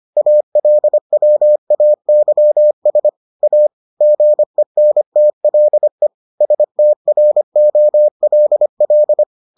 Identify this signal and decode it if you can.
Morse code, heard as a keyed tone, about 25 wpm, 600 Hz. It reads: ALWAYS A GENTLE STROLL